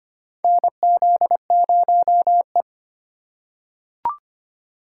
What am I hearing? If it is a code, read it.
Morse code, heard as a keyed tone, about 25 wpm, 700 Hz. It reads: NZ0E
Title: USA Call Signs - 4x Character Spacing - 25wpm